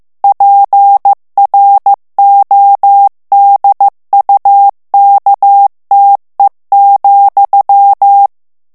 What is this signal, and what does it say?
Morse code, heard as a keyed tone, about 15 wpm, 800 Hz. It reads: PRODUKTE,
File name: morse mp3 / ALLEMAND texte clair 1 vitesse 14 mots minute